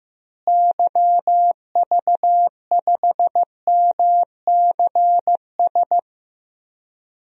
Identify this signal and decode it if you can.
Morse code, heard as a keyed tone, about 15 wpm, 700 Hz. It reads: YV5MCS
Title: DXCC-100 Call Signs - 6 Characters - Repeated 2x - 15wpm